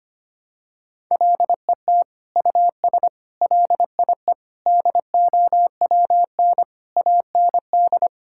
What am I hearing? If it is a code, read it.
Morse code, heard as a keyed tone, about 25 wpm, 700 Hz. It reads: LET US LIE DOWN AND
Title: Sentences from Top 1600 Words 25wpm